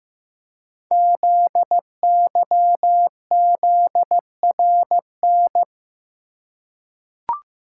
Morse code, heard as a keyed tone, about 15 wpm, 700 Hz. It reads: ZYZRN
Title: Code Groups - 5 Letters - 15wpm